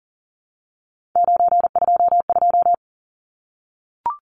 Morse code, heard as a keyed tone, about 40 wpm, 700 Hz. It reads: 922
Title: Numbers - 3-Digit 40wpm